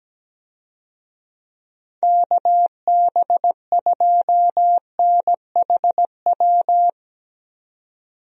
Morse code, heard as a keyed tone, about 17 wpm, 700 Hz. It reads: KB2NHW
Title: USA Call Signs - 6 Characters - 17wpm